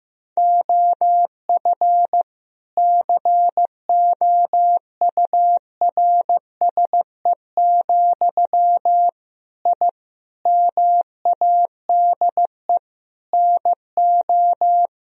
Morse code, heard as a keyed tone, about 15 wpm, 700 Hz. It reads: OF COURSE, I MADE NO
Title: Sentences from Top 1000 Words - Review Part 1 - 15wpm